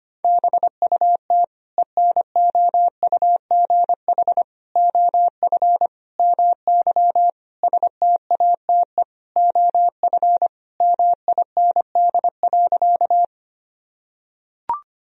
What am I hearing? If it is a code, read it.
Morse code, heard as a keyed tone, about 25 wpm, 700 Hz. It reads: BUT ENOUGH OF MY STATE OF MIND.
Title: Sentences from Top 1000 Words - Review Part 2 - 25wpm